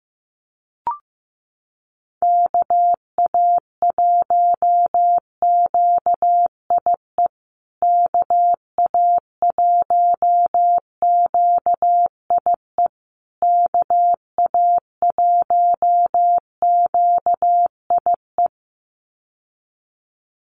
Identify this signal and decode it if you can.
Morse code, heard as a keyed tone, about 15 wpm, 700 Hz. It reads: KA1QIE KA1QIE KA1QIE
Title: USA Call Signs Repeated 3x 15wpm